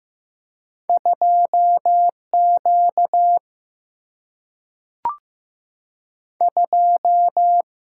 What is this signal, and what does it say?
Morse code, heard as a keyed tone, about 15 wpm, 700 Hz. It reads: 2 Q 2
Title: Single-Letter Single-Number 15wpm